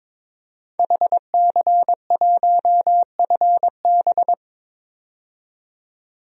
Morse code, heard as a keyed tone, about 22 wpm, 700 Hz. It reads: HC1FB
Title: DXCC-100 Call Signs - 5 Characters - 22wpm